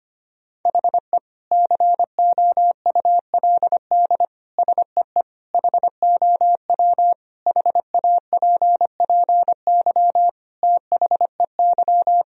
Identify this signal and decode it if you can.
Morse code, heard as a keyed tone, about 25 wpm, 700 Hz. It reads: HE COULD SEE HOW HAPPY THEY
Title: Sentences from Top 1000 Words - Review Part 3 - 25wpm